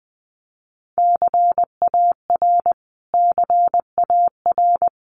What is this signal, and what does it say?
Morse code, heard as a keyed tone, about 20 wpm, 700 Hz. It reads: CAR CAR
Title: Top 500 Words Repeated 20wpm